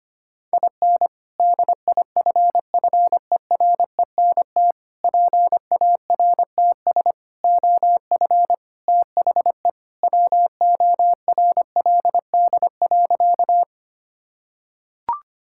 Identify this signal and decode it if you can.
Morse code, heard as a keyed tone, about 25 wpm, 700 Hz. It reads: IN DIFFERENT PARTS OF THE WORLD.
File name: practice / 25wpm / Sentences with 3-Word Phrases - 25wpm